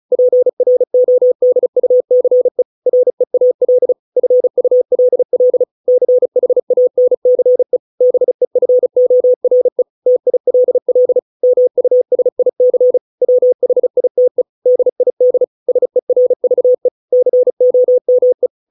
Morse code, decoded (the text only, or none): PRODUCE REAL FULL CHANCE BEFORE TILL MUSIC WHITE DID SERVE COME